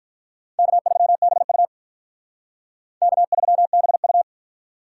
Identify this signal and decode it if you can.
Morse code, heard as a keyed tone, about 53 wpm, 700 Hz. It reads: K3BU K3BU